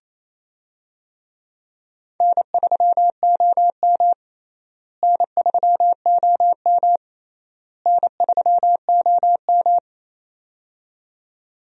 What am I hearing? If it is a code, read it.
Morse code, heard as a keyed tone, about 28 wpm, 700 Hz. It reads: N3OM N3OM N3OM
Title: USA Call Signs - 4 Characters - Repeated 3x - 28wpm